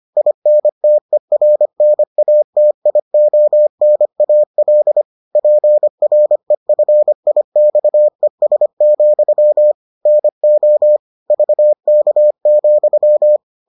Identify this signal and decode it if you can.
Morse code, heard as a keyed tone, about 25 wpm, 600 Hz. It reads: INTERNATIONAL PREFIXES, NO VK,